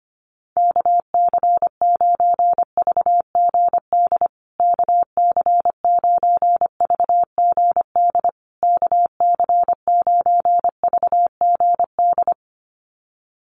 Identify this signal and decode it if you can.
Morse code, heard as a keyed tone, about 25 wpm, 700 Hz. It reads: KC9VGD KC9VGD KC9VGD
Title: USA Call Signs Repeated 3x 25wpm